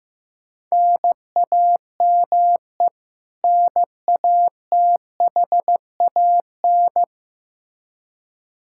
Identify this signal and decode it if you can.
Morse code, heard as a keyed tone, about 15 wpm, 700 Hz. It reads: NAME NATHAN
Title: Practice QSO - Exchange 1 - Beginning - 15wpm